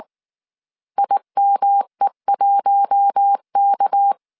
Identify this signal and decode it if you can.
Morse code, heard as a keyed tone, about 19 wpm, 800 Hz. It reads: IME1K
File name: CW / DET / 1 6 1500C S C 18wpm 2356